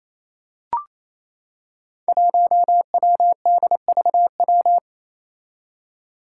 Morse code, heard as a keyed tone, about 28 wpm, 700 Hz. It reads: 1WDVW
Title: Five Challenging Letter-Number Combinations 28wpm